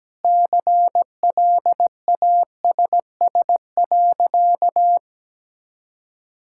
Morse code, heard as a keyed tone, about 17 wpm, 700 Hz. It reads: CLASS.
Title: Sentences from Top 1000 Words - Review Part 6 - 17wpm